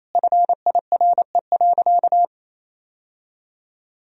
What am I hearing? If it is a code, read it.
Morse code, heard as a keyed tone, about 28 wpm, 700 Hz. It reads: FIRE.
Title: Sentences from Top 1000 Words - Review Part 4 - 28wpm